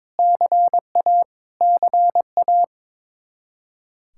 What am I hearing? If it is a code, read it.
Morse code, heard as a keyed tone, about 21 wpm, 700 Hz. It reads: CA CA
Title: Practice QSO - Exchange 1 - Beginning - 22wpm